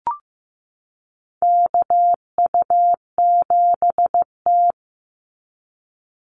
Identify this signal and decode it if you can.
Morse code, heard as a keyed tone, about 15 wpm, 700 Hz. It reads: KU7T
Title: Call Signs Repeated 3x - 15wpm